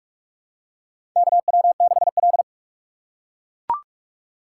Morse code, heard as a keyed tone, about 43 wpm, 700 Hz. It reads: KW6L
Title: USA Call Signs - 4 Characters - 2x Character Spacing - 45wpm